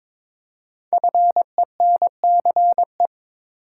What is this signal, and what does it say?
Morse code, heard as a keyed tone, about 22 wpm, 700 Hz. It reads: FENCE
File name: 5 Letter Words - First 3 Letters Given - 22wpm